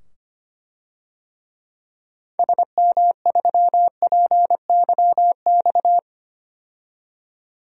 Morse code, heard as a keyed tone, about 25 wpm, 700 Hz. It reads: SM3PYX
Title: DXCC-100 Call Signs - 6 Characters - 25wpm